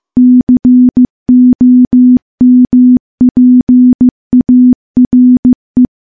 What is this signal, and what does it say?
Morse code, heard as a keyed tone, about 15 wpm, 250 Hz. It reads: COMPARE
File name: morse-training-15WPM